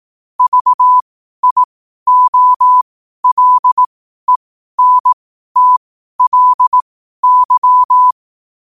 Morse code, heard as a keyed tone, about 18 wpm, spaced as for 12 wpm, 1000 Hz. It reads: VIOLENTLY
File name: qotdfiles / 12wpm